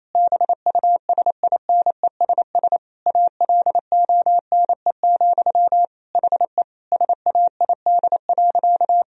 Morse code, heard as a keyed tone, about 28 wpm, 690 Hz. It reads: BUSINESS ALONE, HE SAID.
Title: Sentences from Top 1000 Words - Review Part 2 - 28wpm